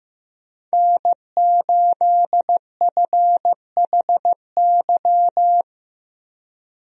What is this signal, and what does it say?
Morse code, heard as a keyed tone, about 15 wpm, 700 Hz. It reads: N8FHY
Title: USA Call Signs - 5 Characters - 15wpm